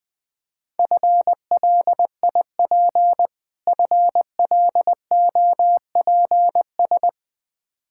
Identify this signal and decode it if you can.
Morse code, heard as a keyed tone, about 20 wpm, 700 Hz. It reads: FLIP FLOPS
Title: Difficult Words with Hints - 20wpm